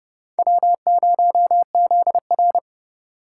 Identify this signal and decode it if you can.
Morse code, heard as a keyed tone, about 26 wpm, 700 Hz. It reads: W0ZR